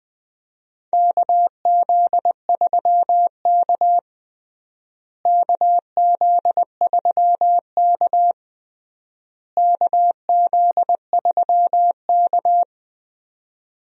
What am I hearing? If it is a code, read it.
Morse code, heard as a keyed tone, about 20 wpm, 700 Hz. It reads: KZ3K KZ3K KZ3K
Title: USA Call Signs - 4 Characters - Repeated 3x - 20wpm